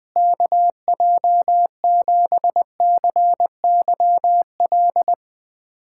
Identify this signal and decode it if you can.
Morse code, heard as a keyed tone, about 21 wpm, 700 Hz. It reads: KJ7CYL
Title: USA Callsign DE Callsign - 6 Characters - 20wpm